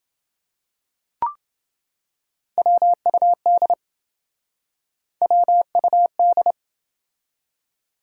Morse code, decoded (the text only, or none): WUD WUD